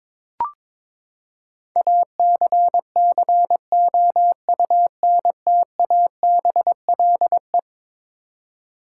Morse code, heard as a keyed tone, about 22 wpm, 700 Hz. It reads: ACCOUNTABLE
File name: practice / 22wpm / Eleven Letter Words - 22wpm